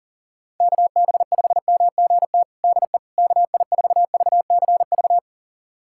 Morse code, heard as a keyed tone, about 37 wpm, 700 Hz. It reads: KB5MGT DE KI4VCV
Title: USA Callsign DE Callsign - 6 Characters - 40wpm